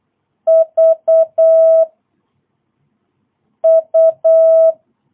Morse code, heard as a keyed tone, about 8 wpm, 660 Hz. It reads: V U
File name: jh8cbh / 2022cw